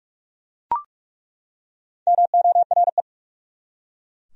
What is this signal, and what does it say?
Morse code, heard as a keyed tone, about 46 wpm, 700 Hz. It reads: MORE
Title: Top 100 Words 45wpm